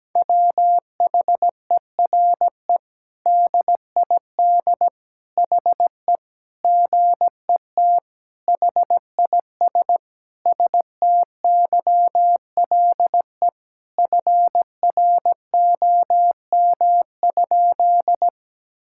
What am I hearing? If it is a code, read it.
Morse code, heard as a keyed tone, about 17 wpm, 700 Hz. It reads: WHERE DID HE GET HIS STYLE FROM?
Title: Sentences from Top 1000 Words - Review Part 5 - 17wpm